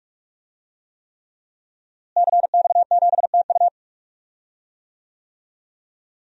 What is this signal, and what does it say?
Morse code, heard as a keyed tone, about 41 wpm, 700 Hz. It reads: CX7TU
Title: DXCC-100 Call Signs - 5 Characters - Repeated 3x - 45wpm